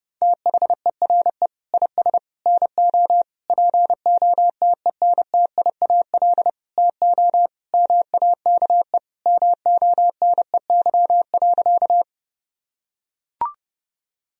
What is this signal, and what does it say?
Morse code, heard as a keyed tone, about 30 wpm, 700 Hz. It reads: THERE IS NO POTENTIAL TO MAKE MONEY.
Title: Sentences from Top 1000 Words - Review Part 4 - 30wpm